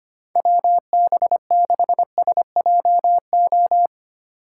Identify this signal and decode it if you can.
Morse code, heard as a keyed tone, about 31 wpm, 700 Hz. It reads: WB6SJO
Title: Speed Racing - USA Call Signs 25wpm